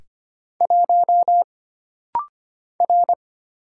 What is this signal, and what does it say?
Morse code, heard as a keyed tone, about 25 wpm, 700 Hz. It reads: 1 R
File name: Single Character (Letter:Number:Punctuation:Prosign) - Rapid-Fire - 25wpm